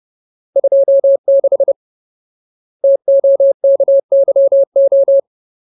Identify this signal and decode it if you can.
Morse code, heard as a keyed tone, about 30 wpm, 550 Hz. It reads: 26 TOKYO
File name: Ultimate Word List / 30WPM Ultimate